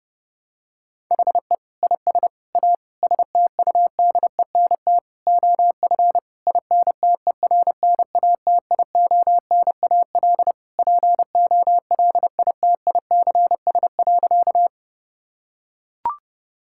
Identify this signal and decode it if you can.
Morse code, heard as a keyed tone, about 30 wpm, 700 Hz. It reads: HE IS A STUDENT OF INTERNATIONAL POLITICS.
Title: Sentences from Top 1000 Words - Review Part 6 - 30wpm